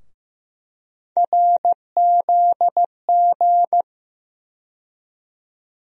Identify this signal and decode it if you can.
Morse code, heard as a keyed tone, about 15 wpm, 700 Hz. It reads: RZG
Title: USA Call Sign Postfixes - 3 Letters 15wpm